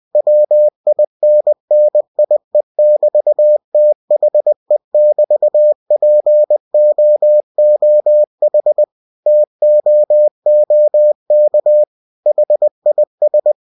Morse code, heard as a keyed tone, about 20 wpm, 600 Hz. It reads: WINNIE<BT>THE<BT>POOH TOOK HIS
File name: cw-books / winnie-the-pooh / winnie-20wpm-page-0014